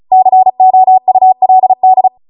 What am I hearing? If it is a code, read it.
Morse code, heard as a keyed tone, about 36 wpm, 750 Hz. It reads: COULD
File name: Morse / 35WPM